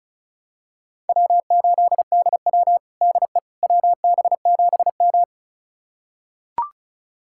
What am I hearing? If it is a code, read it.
Morse code, heard as a keyed tone, about 34 wpm, 700 Hz. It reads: W8DW DE WB7M
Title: USA Callsign DE Callsign - 4 Characters - 35wpm